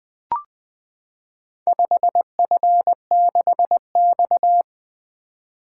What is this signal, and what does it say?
Morse code, heard as a keyed tone, about 20 wpm, 700 Hz. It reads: 5F6X
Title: Four Letter-Number Combinations - 20wpm